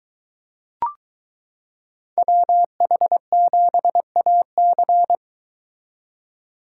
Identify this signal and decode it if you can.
Morse code, heard as a keyed tone, about 23 wpm, 700 Hz. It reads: WH7AC